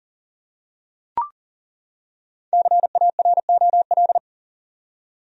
Morse code, heard as a keyed tone, about 40 wpm, 700 Hz. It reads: CAROL
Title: USA Common Names - Max 5 Letters - 40wpm